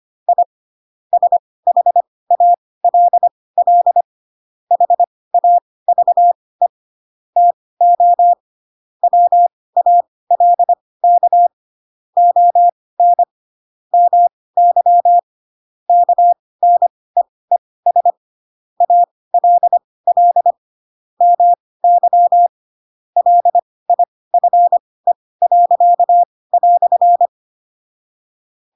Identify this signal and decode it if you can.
Morse code, heard as a keyed tone, about 25 wpm, spaced as for 18 wpm, 700 Hz. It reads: I SHALL HAVE TO WALK ON MY KNEES ALL MY LIFE."
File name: Pinocchio / Pinocchio ch7 12 25 ik1hge